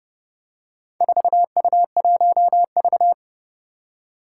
Morse code, heard as a keyed tone, about 29 wpm, 690 Hz. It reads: 4U1V
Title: Four Challenging Letter-Number Combinations 30wpm